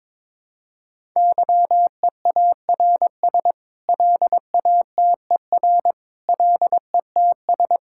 Morse code, heard as a keyed tone, about 22 wpm, 700 Hz. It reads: YEARS LATER LETS
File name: practice / 22wpm / Sets of 8 Words - 22wpm